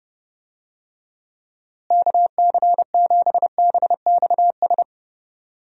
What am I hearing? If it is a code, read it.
Morse code, heard as a keyed tone, about 30 wpm, 700 Hz. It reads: KC7BXS
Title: USA Call Signs - 6 Characters - Repeated 3x - 30wpm